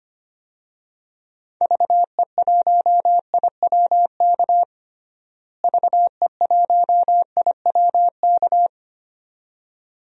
Morse code, heard as a keyed tone, about 25 wpm, 700 Hz. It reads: VE1IWK VE1IWK
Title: DXCC-100 Call Signs - 6 Characters - Repeated 2x - 25wpm